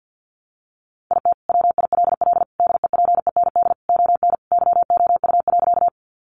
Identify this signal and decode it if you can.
Morse code, heard as a keyed tone, about 50 wpm, 700 Hz. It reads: IT WILL DEPEND ON YOU.
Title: Sentences from Top 1300 Words 50wpm